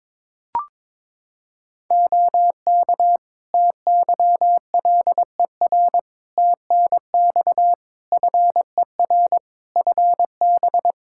Morse code, heard as a keyed tone, about 22 wpm, 700 Hz. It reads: OK TYLER TNX FER FB
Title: Practice QSO - Exchange 4 - Ending - 22wpm